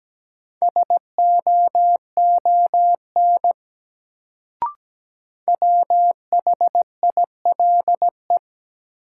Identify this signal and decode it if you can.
Morse code, heard as a keyed tone, about 17 wpm, 700 Hz. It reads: SOON WHILE
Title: Sets of 3 Words - New Word Drill - 17wpm